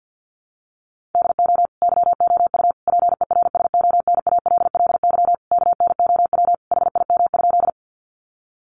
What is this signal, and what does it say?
Morse code, heard as a keyed tone, about 50 wpm, 700 Hz. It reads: DO YOU PERSONALLY KNOW HIM?
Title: Sentences from Top 1400 Words 50wpm